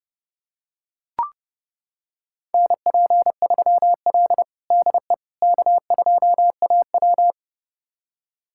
Morse code, heard as a keyed tone, about 32 wpm, 700 Hz. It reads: NP3L DE K2AW
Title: USA Callsign DE Callsign - 4 Characters - 30wpm